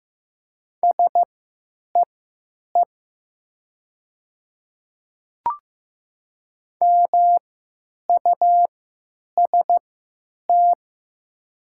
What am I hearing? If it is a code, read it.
Morse code, heard as a keyed tone, about 15 wpm, spaced as for 9 wpm, 700 Hz. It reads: SEE MUST
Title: 3x Character Spacing - Top 300 Words - 15wpm